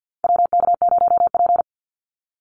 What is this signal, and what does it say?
Morse code, heard as a keyed tone, about 50 wpm, 700 Hz. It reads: WK0P
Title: Call Signs Repeated 2x - 50wpm